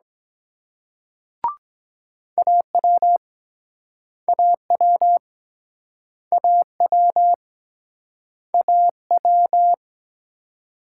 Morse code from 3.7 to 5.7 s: AW

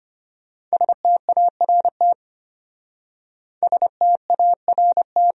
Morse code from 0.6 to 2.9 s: START

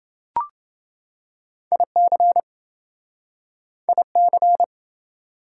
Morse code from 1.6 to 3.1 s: IC

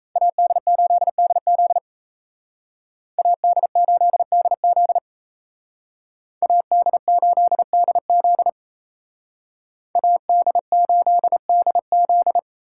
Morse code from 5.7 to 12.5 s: AD8DZ AD8DZ

